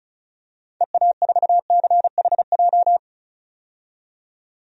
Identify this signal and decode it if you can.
Morse code, heard as a keyed tone, about 36 wpm, 700 Hz. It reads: EA4CHJ
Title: DXCC-100 Call Signs - 6 Characters - 35wpm